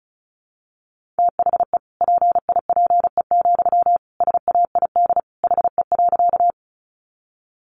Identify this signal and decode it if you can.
Morse code, heard as a keyed tone, about 34 wpm, 700 Hz. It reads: THE PIPE, SAID HE.
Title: Sentences from Top 1500 Words 35wpm